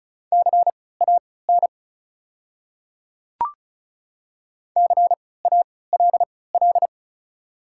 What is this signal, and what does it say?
Morse code, heard as a keyed tone, about 36 wpm, spaced as for 20 wpm, 700 Hz. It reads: CAN CALL
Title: 3x Character Spacing - Top 100 Words - 35wpm